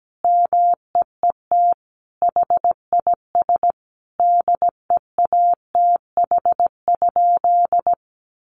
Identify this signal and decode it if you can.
Morse code, heard as a keyed tone, about 17 wpm, 700 Hz. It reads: MEET HIS DEATH?
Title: Sentences from Top 1100 Words 17wpm